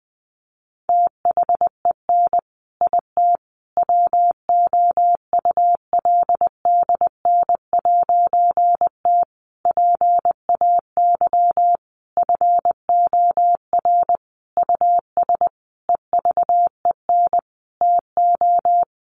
Morse code, heard as a keyed tone, about 20 wpm, 700 Hz. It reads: THEN IT WOULDN'T PAY FOR US EVEN TO
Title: A Sound Of Thunder 20wpm